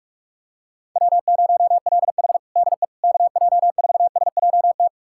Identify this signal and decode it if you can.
Morse code, heard as a keyed tone, about 46 wpm, 700 Hz. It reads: W0LH DE KJ4SJT
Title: USA Callsign DE Callsign 45wpm